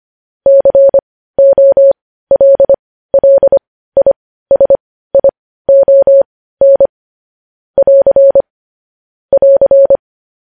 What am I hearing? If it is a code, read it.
Morse code, heard as a keyed tone, about 25 wpm, spaced as for 15 wpm, 550 Hz. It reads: COLLISION <AR> <AR>
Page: news headlines in morse code at 15 wpm